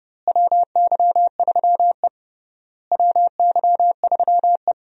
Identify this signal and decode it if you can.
Morse code, heard as a keyed tone, about 30 wpm, 700 Hz. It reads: WY3E WY3E
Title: USA Call Signs - 4 Characters - Repeated 3x - 30wpm